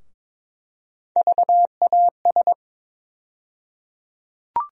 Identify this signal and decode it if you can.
Morse code, heard as a keyed tone, about 22 wpm, 700 Hz. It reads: VAS
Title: USA Call Sign Postfixes - 3 Letters 22wpm